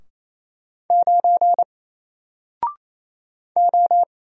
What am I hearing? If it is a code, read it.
Morse code, heard as a keyed tone, about 27 wpm, 700 Hz. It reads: 9 O
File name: Single-Letter-Number-28wpm